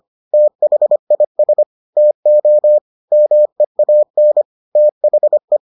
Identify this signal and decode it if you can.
Morse code, heard as a keyed tone, about 25 wpm, 600 Hz. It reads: THIS TO MEAN THE